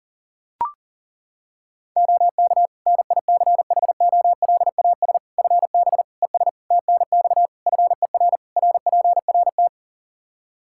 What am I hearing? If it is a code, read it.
Morse code, heard as a keyed tone, about 37 wpm, 700 Hz. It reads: OK NICHOLAS FB ES TNX FER RPRT
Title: Practice QSO - Exchange 2 - Info - 40wpm